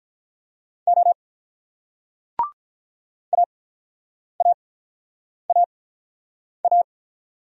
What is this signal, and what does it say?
Morse code, heard as a keyed tone, about 48 wpm, spaced as for 9 wpm, 700 Hz. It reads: O AAAA